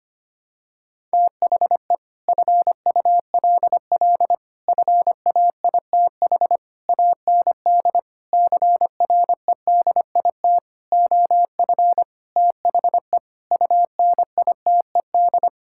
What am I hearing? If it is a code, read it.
Morse code, heard as a keyed tone, about 25 wpm, 700 Hz. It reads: THE FULL FAITH AND CREDIT OF THE UNITED
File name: Sets of 9 Words - 25wpm